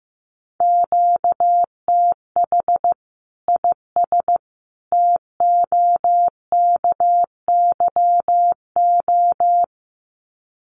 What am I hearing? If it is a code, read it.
Morse code, heard as a keyed tone, about 15 wpm, 700 Hz. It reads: QTH IS TOKYO
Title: Common QSO Phrases 15wpm